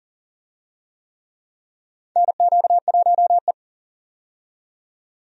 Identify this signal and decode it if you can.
Morse code, heard as a keyed tone, about 40 wpm, 700 Hz. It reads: NQ1E
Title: USA Call Signs - 4 Characters - 40wpm